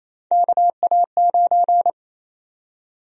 Morse code, heard as a keyed tone, about 28 wpm, 700 Hz. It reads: KA9
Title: USA Call Sign Prefixes - 28wpm